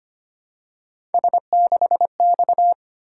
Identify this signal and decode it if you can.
Morse code, heard as a keyed tone, about 25 wpm, 700 Hz. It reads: S6X
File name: Three Letter-Number Combinations - 25wpm